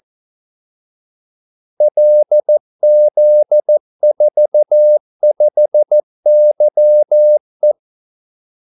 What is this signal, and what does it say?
Morse code, heard as a keyed tone, about 14 wpm, 600 Hz. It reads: LZ45YE
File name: Callsigns-14wpm